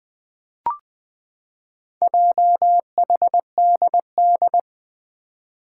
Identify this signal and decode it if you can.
Morse code, heard as a keyed tone, about 20 wpm, 700 Hz. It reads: JHDD
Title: Four Challenging Letter-Number Combinations 20wpm